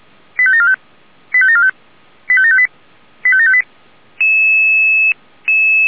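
Five-tone selective call (ZVEI), heard as quick runs of five tones, tones of about 70 ms. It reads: 85745 85745 85758 85758
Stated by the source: ZVEI five-tone call received on SDR.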